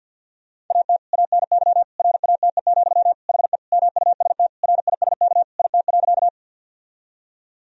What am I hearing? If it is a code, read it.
Morse code, heard as a keyed tone, about 51 wpm, 700 Hz. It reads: AT ANY RATE, HE MUST RISK IT.
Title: Sentences from Top 1000 Words - Review Part 4 - 50wpm